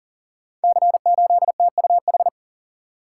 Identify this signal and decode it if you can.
Morse code, heard as a keyed tone, about 38 wpm, 690 Hz. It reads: C8TUH
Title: Code Groups - 5 Letter:Numbers - 40wpm